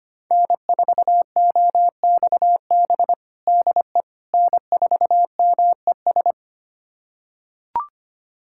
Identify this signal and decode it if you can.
Morse code, heard as a keyed tone, about 25 wpm, 700 Hz. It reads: N4OXB DE N4MES
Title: USA Callsign DE Callsign - 5 Characters - 25wpm